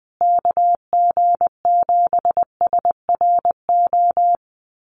Morse code, heard as a keyed tone, about 20 wpm, 700 Hz. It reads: KG7SRO
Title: USA Call Signs Repeated 3x 20wpm